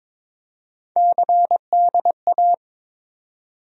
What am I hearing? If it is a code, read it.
Morse code, heard as a keyed tone, about 22 wpm, 700 Hz. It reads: CDA
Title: USA Call Sign Postfixes - 3 Letters 22wpm